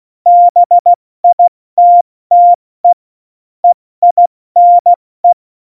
Morse code, heard as a keyed tone, about 16 wpm, spaced as for 14 wpm, 700 Hz. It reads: BITTE EINE
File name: Morsekurs / Witze / Witz 09. April 2019, 70 F80